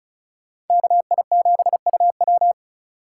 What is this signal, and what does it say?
Morse code, heard as a keyed tone, about 34 wpm, 700 Hz. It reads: KI7UW
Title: USA Call Signs - 2x Character Spacing - 35wpm